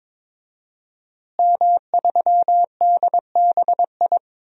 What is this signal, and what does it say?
Morse code, heard as a keyed tone, about 22 wpm, 700 Hz. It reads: M3DBI
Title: DXCC-100 Call Signs - 5 Characters - Repeated 2x - 22wpm